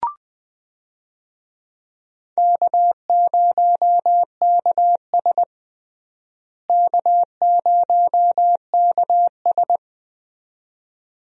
Morse code, heard as a keyed tone, about 20 wpm, 700 Hz. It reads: K0KS K0KS
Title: USA Call Signs - 4 Characters - Repeated 2x - 20wpm